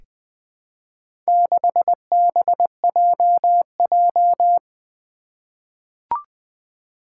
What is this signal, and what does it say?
Morse code, heard as a keyed tone, about 20 wpm, 700 Hz. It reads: 6BJJ
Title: Four Challenging Letter-Number Combinations 20wpm